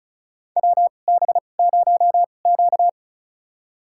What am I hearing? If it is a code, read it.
Morse code, heard as a keyed tone, about 34 wpm, spaced as for 25 wpm, 690 Hz. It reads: WB0Q
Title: USA Call Signs - 4 Characters - 2x Character Spacing - 35wpm